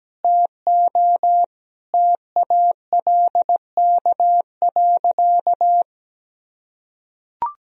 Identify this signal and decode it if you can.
Morse code, heard as a keyed tone, about 17 wpm, 700 Hz. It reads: TO TALK.
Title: Sentences from Top 1000 Words - Review Part 1 - 17wpm